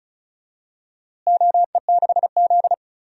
Morse code, heard as a keyed tone, about 36 wpm, 700 Hz. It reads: OE6Z
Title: DXCC-100 Call Signs - 4 Characters - Repeated 3x - 35wpm